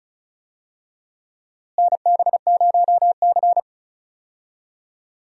Morse code, heard as a keyed tone, about 36 wpm, 700 Hz. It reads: NB0C
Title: USA Call Signs - 4 Characters - Repeated 3x - 35wpm